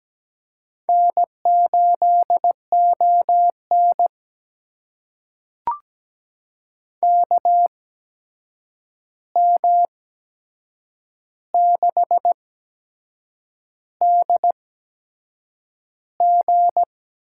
Morse code, heard as a keyed tone, about 17 wpm, spaced as for 7 wpm, 700 Hz. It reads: N8ON KM6DG
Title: USA Call Signs - 8x Character Spacing - 17wpm